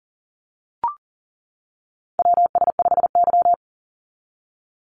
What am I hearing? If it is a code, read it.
Morse code, heard as a keyed tone, about 40 wpm, 700 Hz. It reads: WS5Y
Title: USA Call Signs 40wpm